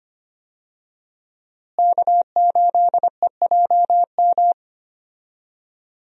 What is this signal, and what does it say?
Morse code, heard as a keyed tone, about 25 wpm, 700 Hz. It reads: K8EJM
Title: USA Call Signs - 5 Characters - Repeated 2x - 25wpm